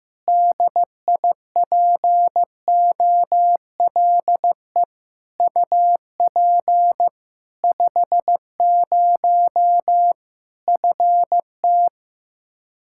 Morse code, heard as a keyed tone, about 15 wpm, 700 Hz. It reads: DIPOLE UP 50 FT
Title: K1USN SST QSO Practice 15wpm